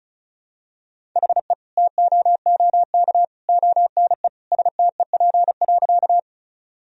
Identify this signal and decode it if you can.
Morse code, heard as a keyed tone, about 34 wpm, 700 Hz. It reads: HE TOOK ONE STEP.
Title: Sentences from Top 1000 Words - Review Part 2 - 35wpm